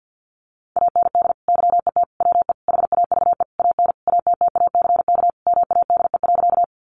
Morse code, heard as a keyed tone, about 50 wpm, 700 Hz. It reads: AND YET WE HAVE AN ATTACK MADE.
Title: Sentences from Top 1500 Words 50wpm